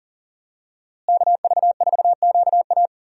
Morse code, heard as a keyed tone, about 40 wpm, 700 Hz. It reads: KV4QA
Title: USA Call Signs - 5 Characters - Repeated 3x - 40wpm